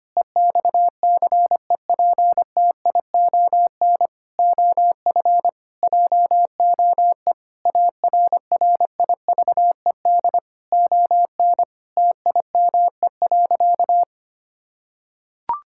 Morse code, heard as a keyed tone, about 25 wpm, 700 Hz. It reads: EXCEPTION OF JOE ARRIVED ON TIME.
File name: practice / 25wpm / Sentences with 4-Word Phrases - 25wpm